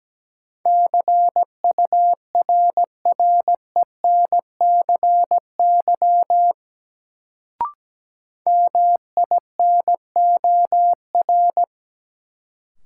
Morse code, heard as a keyed tone, about 17 wpm, 700 Hz. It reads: CURRENCY MINOR
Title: Sentences from Top 1500 Words - New Word Drill Alt - 17wpm